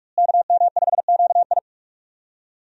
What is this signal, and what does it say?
Morse code, heard as a keyed tone, about 46 wpm, 700 Hz. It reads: KM5QI
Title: USA Call Signs - 5 Characters - Repeated 3x - 45wpm